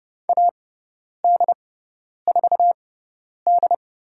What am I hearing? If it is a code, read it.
Morse code, heard as a keyed tone, about 30 wpm, spaced as for 10 wpm, 700 Hz. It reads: AD4D
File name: Lesson 17g - Letter u - Call Signs 30wpm